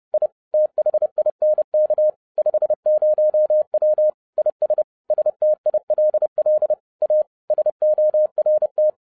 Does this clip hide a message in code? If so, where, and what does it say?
Yes: Morse code from 0.1 to 8.9 s, heard as a keyed tone, about 30 wpm, 610 Hz. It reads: I THINK 50W IS STILL A SORT